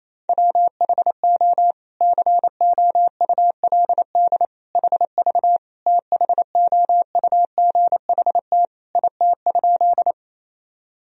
Morse code, heard as a keyed tone, about 28 wpm, 700 Hz. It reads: WHO COULD HV THOUGHT IT?
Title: Sentences from Top 300 Words and Abbreviations 28wpm